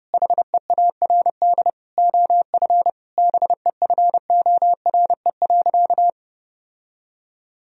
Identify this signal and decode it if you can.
Morse code, heard as a keyed tone, about 30 wpm, 700 Hz. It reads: HEARD OF BEFORE.
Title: Sentences from Top 1000 Words - Review Part 3 - 30wpm